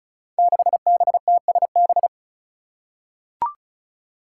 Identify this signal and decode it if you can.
Morse code, heard as a keyed tone, about 36 wpm, 700 Hz. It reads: 6BTSB
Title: Five Challenging Letter-Number Combinations 35wpm